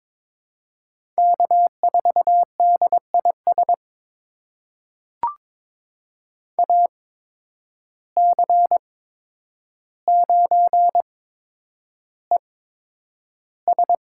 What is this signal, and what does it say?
Morse code, heard as a keyed tone, about 22 wpm, spaced as for 9 wpm, 700 Hz. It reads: K4DIS AC9ES
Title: USA Call Signs - 8x Character Spacing - 22wpm